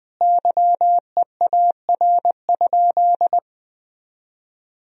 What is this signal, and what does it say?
Morse code, heard as a keyed tone, about 20 wpm, 700 Hz. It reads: YEAR?
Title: Sentences from Top 1000 Words - Review Part 4 - 20wpm